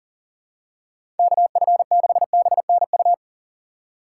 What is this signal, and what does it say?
Morse code, heard as a keyed tone, about 40 wpm, 700 Hz. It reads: KF6BNU
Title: USA Call Signs - 3x Character Spacing - 40wpm